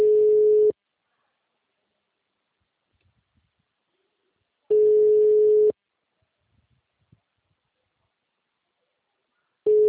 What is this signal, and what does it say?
Ringback tone (Europe, 425 Hz).